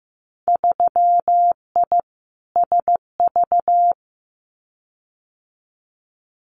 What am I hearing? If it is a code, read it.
Morse code, heard as a keyed tone, about 15 wpm, 700 Hz. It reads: 3I SV
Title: Winter Field Day Exchanges Encore 15wpm